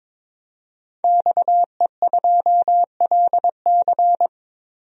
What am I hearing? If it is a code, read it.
Morse code, heard as a keyed tone, about 22 wpm, 700 Hz. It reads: XE2LC
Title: DXCC-100 Call Signs - 5 Characters - 22wpm